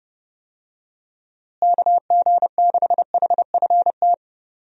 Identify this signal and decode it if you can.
Morse code, heard as a keyed tone, about 30 wpm, 700 Hz. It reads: KG6HFT
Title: USA Call Signs - 6 Characters - Repeated 3x - 30wpm